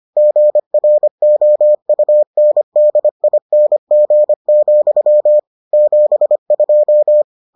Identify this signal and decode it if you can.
Morse code, heard as a keyed tone, about 25 wpm, 600 Hz. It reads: GROUNDING, 72